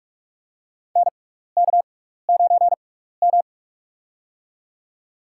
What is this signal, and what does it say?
Morse code, heard as a keyed tone, about 44 wpm, spaced as for 15 wpm, 700 Hz. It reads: NK9M
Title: USA Call Signs - 4 Characters - 6x Character Spacing - 45wpm